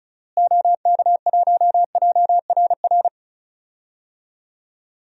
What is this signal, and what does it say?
Morse code, heard as a keyed tone, about 36 wpm, 700 Hz. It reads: OK1JRR
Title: DXCC-100 Call Signs - 6 Characters - Repeated 3x - 35wpm